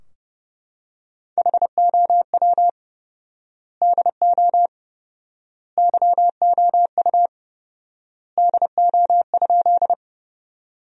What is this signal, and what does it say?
Morse code, heard as a keyed tone, about 30 wpm, 700 Hz. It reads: HOW DO YOU DO?
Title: 4x Word-Spacing - Sentences from Top 100 Words - 30wpm